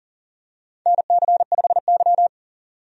Morse code, heard as a keyed tone, about 40 wpm, 700 Hz. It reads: NC5Y
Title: USA Call Signs - 4 Characters - 40wpm